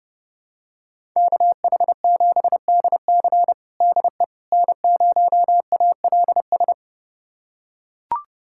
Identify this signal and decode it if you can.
Morse code, heard as a keyed tone, about 32 wpm, 700 Hz. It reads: KH7DC DE N0ALS
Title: USA Callsign DE Callsign - 5 Characters - 30wpm